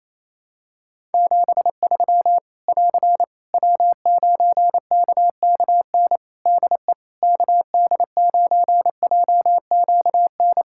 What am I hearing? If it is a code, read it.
Morse code, heard as a keyed tone, about 28 wpm, 700 Hz. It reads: 73 <AR> W9KKN DE KD9JQN
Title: Practice QSO - Exchange 4 - Ending - 28wpm